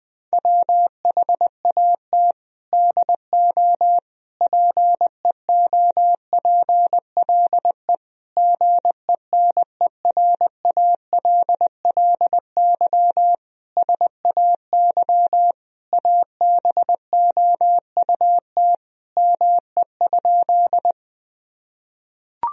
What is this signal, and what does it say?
Morse code, heard as a keyed tone, about 20 wpm, 700 Hz. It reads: WHAT DO PEOPLE GENERALLY SAY ABOUT ME?
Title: Sentences from Top 1000 Words - Review Part 4 - 20wpm